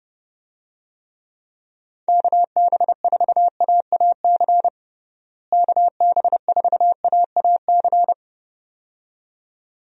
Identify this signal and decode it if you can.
Morse code, heard as a keyed tone, about 30 wpm, 700 Hz. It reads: KB4AAC KB4AAC
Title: USA Call Signs - 6 Characters - Repeated 2x - 30wpm